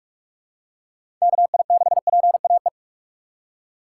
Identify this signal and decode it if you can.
Morse code, heard as a keyed tone, about 45 wpm, 700 Hz. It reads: KI6PAE
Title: USA Call Signs - 6 Characters - 6x Character Spacing - 45wpm